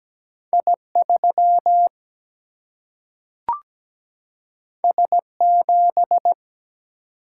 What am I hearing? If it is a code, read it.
Morse code, heard as a keyed tone, about 17 wpm, 700 Hz. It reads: I 3 S 7
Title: Single-Letter Single-Number 17wpm